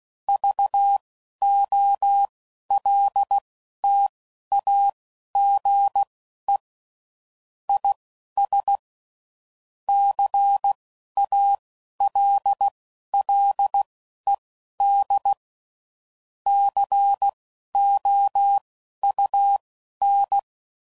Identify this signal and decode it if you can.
Morse code, heard as a keyed tone, about 16 wpm, spaced as for 11 wpm, 800 Hz. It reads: VOLTAGE IS CALLED COUN
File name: CW Code Groups @10WPM / Technical Subjects / XL